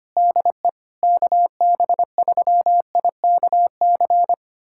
Morse code, heard as a keyed tone, about 25 wpm, 700 Hz. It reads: DE KB3IKC
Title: USA Callsign DE Callsign 25wpm